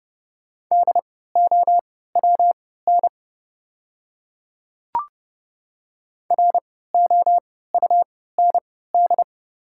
Morse code, heard as a keyed tone, about 30 wpm, spaced as for 17 wpm, 700 Hz. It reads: DOWN ROUND
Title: 3x Character Spacing - Top 200 Words - 30wpm